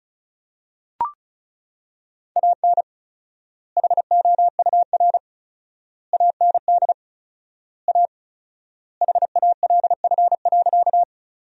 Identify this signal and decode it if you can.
Morse code, heard as a keyed tone, about 36 wpm, 700 Hz. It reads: AN HOUR AND A HALF.
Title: 4x Word-Spacing - Sentences from Top 400 Words - 35wpm